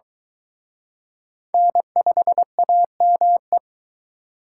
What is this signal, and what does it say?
Morse code, heard as a keyed tone, about 23 wpm, 700 Hz. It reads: N5AME